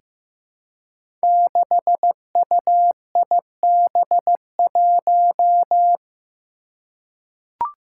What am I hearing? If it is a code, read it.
Morse code, heard as a keyed tone, about 15 wpm, 700 Hz. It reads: 6UIB1
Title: Five Challenging Letter-Number Combinations 15wpm